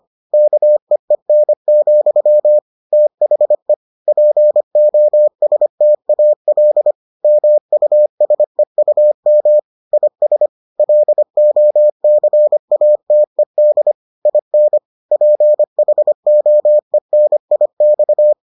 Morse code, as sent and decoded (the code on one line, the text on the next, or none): -.- . -. --..-- / - .... / .--. --- ... - .- .-.. / -- ..- ... ..- -- / .. ... / .-.. --- -.-. .- - . -.. / .. -. / .--. .... --- . -. .. -..-
KEEN, THE POSTAL MUSEUM IS LOCATED IN PHOENIX